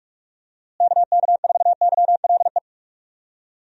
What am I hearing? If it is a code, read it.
Morse code, heard as a keyed tone, about 46 wpm, 700 Hz. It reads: KK4YLE
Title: USA Call Signs - 6 Characters - Repeated 2x - 45wpm